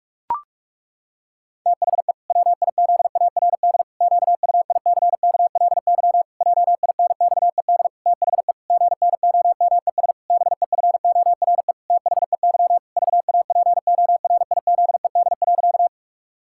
Morse code, heard as a keyed tone, about 46 wpm, 700 Hz. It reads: THE WIZARD QUICKLY JINXED THE GNOMES BEFORE THEY VAPORIZED.
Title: Pangram Challenge v2 - 45wpm